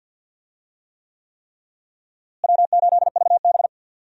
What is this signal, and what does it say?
Morse code, heard as a keyed tone, about 47 wpm, 700 Hz. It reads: W8VB
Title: USA Call Signs - 4 Characters - Repeated 3x - 50wpm